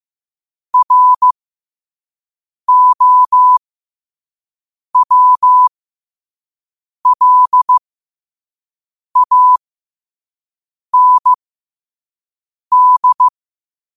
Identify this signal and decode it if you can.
Morse code, heard as a keyed tone, about 15 wpm, spaced as for 5 wpm, 1000 Hz. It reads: ROWLAND
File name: qotdfiles / 5wpm